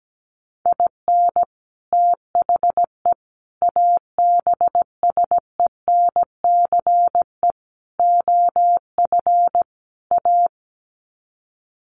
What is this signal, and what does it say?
Morse code, heard as a keyed tone, about 17 wpm, 700 Hz. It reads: IN THE ABSENCE OF A
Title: Sets of 5 Words Encore 17wpm